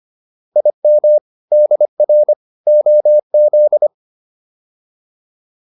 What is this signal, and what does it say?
Morse code, heard as a keyed tone, about 25 wpm, 600 Hz. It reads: IM DR OZ